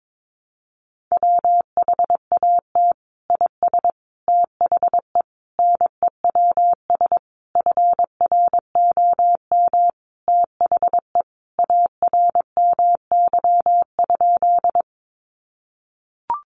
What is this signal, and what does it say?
Morse code, heard as a keyed tone, about 22 wpm, 700 Hz. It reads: WHAT IS THE NEWS FROM THE ARMY?
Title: Sentences from Top 800 Words 22wpm